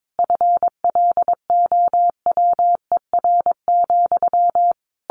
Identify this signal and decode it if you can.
Morse code, heard as a keyed tone, about 22 wpm, 700 Hz. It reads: FLOWER,
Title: A Sound Of Thunder 22wpm